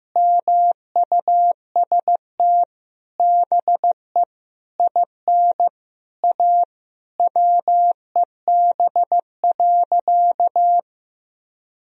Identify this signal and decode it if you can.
Morse code, heard as a keyed tone, about 15 wpm, 700 Hz. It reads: MUST BE IN A WEB.
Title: Sentences from Top 1000 Words - Review Part 4 - 15wpm